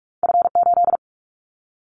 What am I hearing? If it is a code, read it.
Morse code, heard as a keyed tone, about 46 wpm, 700 Hz. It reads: F8
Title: Callsign Prefixes for Top 50 DXCC - 45wpm